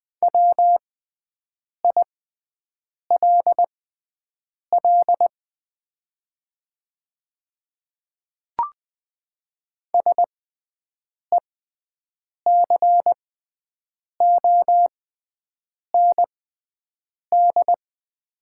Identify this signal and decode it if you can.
Morse code, heard as a keyed tone, about 20 wpm, spaced as for 7 wpm, 700 Hz. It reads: WILL SECOND
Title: 6x Character Spacing - Top 500 Words - 20wpm